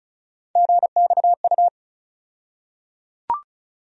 Morse code, heard as a keyed tone, about 34 wpm, 700 Hz. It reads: GXU
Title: USA Call Sign Postfixes - 3 Letters 35wpm